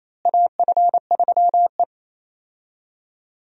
Morse code, heard as a keyed tone, about 28 wpm, 700 Hz. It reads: AF3E
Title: USA Call Signs - 4 Characters - Repeated 3x - 28wpm